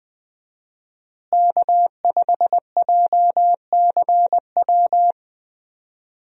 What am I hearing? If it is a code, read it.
Morse code, heard as a keyed tone, about 20 wpm, 700 Hz. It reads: K5JCW
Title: USA Call Signs - 5 Characters - Repeated 3x - 20wpm